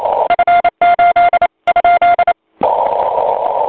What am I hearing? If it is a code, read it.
Morse code, heard as a keyed tone, about 28 wpm, 1400 Hz, with noise. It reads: F8 ?
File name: voice / 80m j3 pa5et 002 2001 08 08